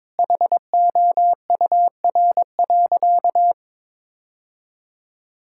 Morse code, heard as a keyed tone, about 21 wpm, 700 Hz. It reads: HOUR.
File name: practice / 22wpm / Sentences with 2-Word Phrases - 22wpm